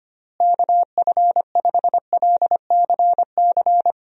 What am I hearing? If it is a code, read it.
Morse code, heard as a keyed tone, about 25 wpm, 700 Hz. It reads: KF5LCC
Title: USA Call Signs - 6 Characters - Repeated 2x - 25wpm